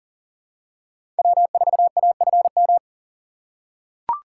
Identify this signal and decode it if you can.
Morse code, heard as a keyed tone, about 40 wpm, 700 Hz. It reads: W4AFM
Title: USA Call Signs - 3x Character Spacing - 40wpm